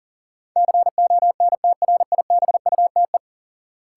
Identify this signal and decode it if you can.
Morse code, heard as a keyed tone, about 40 wpm, 700 Hz. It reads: CONTRIBUTE
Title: Sentences from Top 1400 Words - New Word Drill - 40wpm